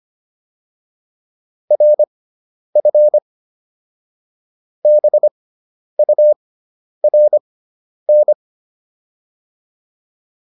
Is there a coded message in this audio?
Yes: Morse code from 1.7 to 8.3 s, heard as a keyed tone, about 25 wpm, spaced as for 10 wpm, 600 Hz. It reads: RF BURN